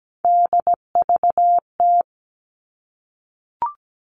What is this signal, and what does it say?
Morse code, heard as a keyed tone, about 17 wpm, 700 Hz. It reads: DVT
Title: All Cut Numbers - 3-Digit 17wpm